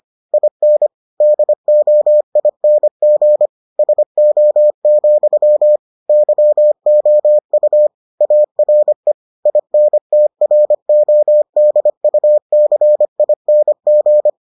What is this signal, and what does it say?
Morse code, heard as a keyed tone, about 25 wpm, 600 Hz. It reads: IN DOING SO, YOU ARE INTRODUCING